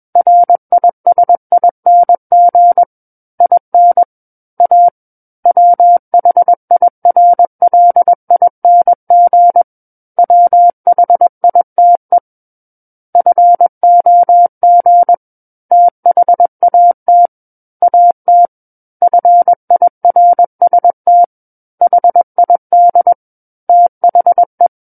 Morse code, heard as a keyed tone, about 21 wpm, 700 Hz. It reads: RISING IN A WHIRLING WHITE FOG THAT AT FIRST HID THE
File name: Book One / 12-Chapter 12 - 21 WPM